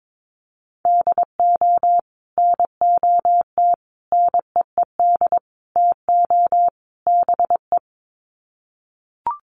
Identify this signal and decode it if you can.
Morse code, heard as a keyed tone, about 22 wpm, 700 Hz. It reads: DO NOT NEED TO BE
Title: Sets of 5 Words Encore 22wpm